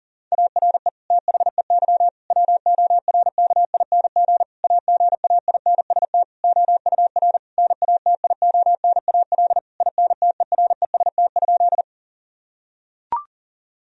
Morse code, heard as a keyed tone, about 40 wpm, 700 Hz. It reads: ARE THEY WORKING AGAINST OUR NATIONAL INTEREST?
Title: Sentences from Top 1000 Words - Review Part 4 - 40wpm